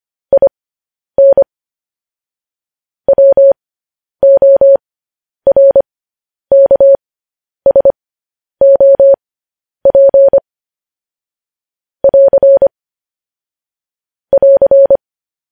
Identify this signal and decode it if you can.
Morse code, heard as a keyed tone, about 25 wpm, spaced as for 10 wpm, 550 Hz. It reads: IN WORKSOP <AR> <AR>
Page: News Headlines in Morse Code at 10 WPM